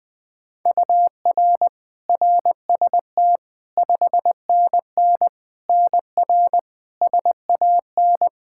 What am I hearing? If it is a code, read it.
Morse code, heard as a keyed tone, about 21 wpm, 700 Hz. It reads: UR RST 5NN NR SAN
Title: Practice QSO - Exchange 1 - Beginning - 20wpm